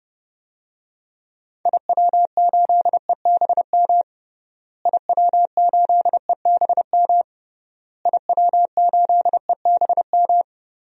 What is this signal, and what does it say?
Morse code, heard as a keyed tone, about 30 wpm, 700 Hz. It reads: IW8EBM IW8EBM IW8EBM
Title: DXCC-100 Call Signs - 6 Characters - Repeated 3x - 30wpm